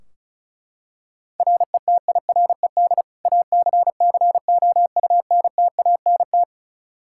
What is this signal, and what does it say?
Morse code, heard as a keyed tone, about 34 wpm, 700 Hz. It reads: RETIRED ACCOUNTANT
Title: Practice QSO - Exchange 3 - More Info - 35wpm